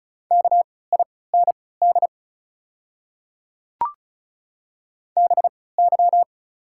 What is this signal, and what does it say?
Morse code, heard as a keyed tone, about 34 wpm, spaced as for 20 wpm, 700 Hz. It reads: KIND BY
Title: 3x Character Spacing - Top 300 Words - 35wpm